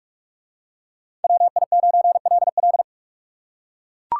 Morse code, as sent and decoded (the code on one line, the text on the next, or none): .-- .. ----. .-.. .-..
WI9LL